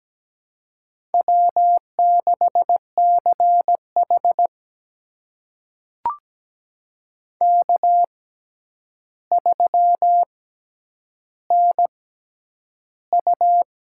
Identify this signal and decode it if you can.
Morse code, heard as a keyed tone, about 17 wpm, spaced as for 9 wpm, 700 Hz. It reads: W6CH K3NU
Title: USA Call Signs - 4 Characters - 6x Character Spacing - 17wpm